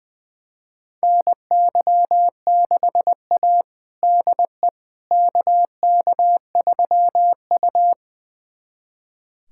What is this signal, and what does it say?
Morse code, heard as a keyed tone, about 21 wpm, 700 Hz. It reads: NY6A DE KK3U
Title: USA Callsign DE Callsign - 4 Characters - 20wpm